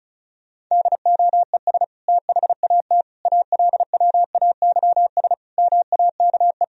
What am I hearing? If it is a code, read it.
Morse code, heard as a keyed tone, about 34 wpm, 700 Hz. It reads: DOES THAT ALWAYS MAKE
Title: Sentences from Top 1000 Words - Review Part 2 - 35wpm